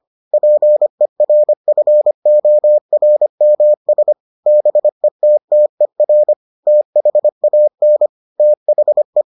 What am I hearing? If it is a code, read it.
Morse code, heard as a keyed tone, about 25 wpm, 600 Hz. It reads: PERFORMS BETTER THAN THE